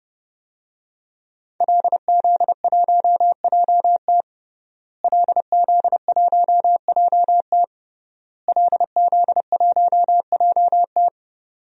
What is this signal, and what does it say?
Morse code, heard as a keyed tone, about 30 wpm, 700 Hz. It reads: LZ1JT LZ1JT LZ1JT
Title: DXCC-100 Call Signs - 5 Characters - Repeated 3x - 30wpm